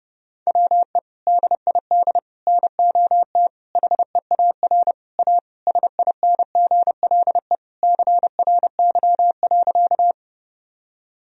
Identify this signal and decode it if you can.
Morse code, heard as a keyed tone, about 30 wpm, 700 Hz. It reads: WE DID NOT HEAR A SINGLE CRY.
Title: Sentences from Top 1000 Words - Review Part 4 - 30wpm